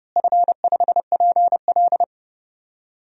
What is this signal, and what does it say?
Morse code, heard as a keyed tone, about 32 wpm, 700 Hz. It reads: F5PL
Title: DXCC-100 Call Signs - 4 Characters - Repeated 2x - 30wpm